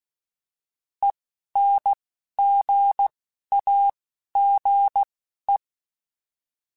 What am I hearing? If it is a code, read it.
Morse code, heard as a keyed tone, about 16 wpm, spaced as for 12 wpm, 800 Hz. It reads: ENGAGE